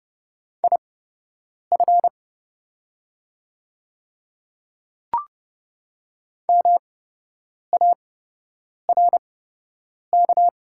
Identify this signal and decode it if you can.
Morse code, heard as a keyed tone, about 30 wpm, spaced as for 8 wpm, 700 Hz. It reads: IF MARK
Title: 8x Character Spacing - Top 300 Words - 30wpm